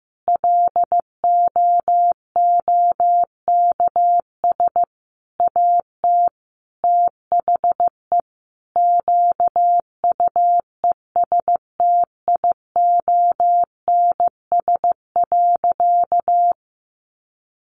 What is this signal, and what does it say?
Morse code, heard as a keyed tone, about 15 wpm, 700 Hz. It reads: LOOKS AT THE QUESTIONS.
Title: Additional Easy Phrases 15wpm